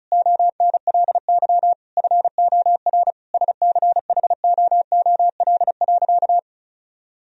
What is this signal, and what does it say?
Morse code, heard as a keyed tone, about 34 wpm, 700 Hz. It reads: ONLY FOR SCHOOL.
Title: Sentences from Top 1000 Words - Review Part 1 - 35wpm